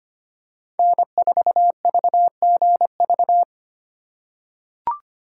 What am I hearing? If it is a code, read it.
Morse code, heard as a keyed tone, about 25 wpm, 700 Hz. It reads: N4VGV
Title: USA Call Signs - 5 Characters - 4x Character Spacing - 25wpm